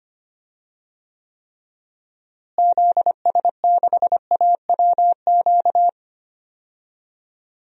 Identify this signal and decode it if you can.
Morse code, heard as a keyed tone, about 25 wpm, 700 Hz. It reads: ZS6AWQ
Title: DXCC-100 Call Signs - 6 Characters - Repeated 3x - 25wpm